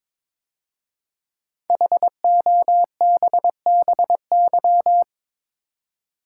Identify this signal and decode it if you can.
Morse code, heard as a keyed tone, about 22 wpm, 700 Hz. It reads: HOBBY
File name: 5 Letter Words - First 3 Letters Given - 22wpm